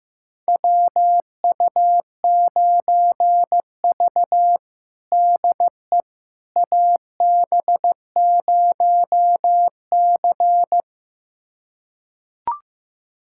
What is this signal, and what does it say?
Morse code, heard as a keyed tone, about 15 wpm, 700 Hz. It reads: WU9V DE AB0C
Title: USA Callsign DE Callsign - 4 Characters - 15wpm